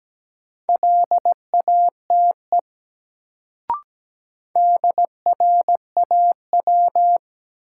Morse code, heard as a keyed tone, about 17 wpm, 700 Hz. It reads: LATE DRAW
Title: Sentences from Top 300 Words - New Word Drill - 17wpm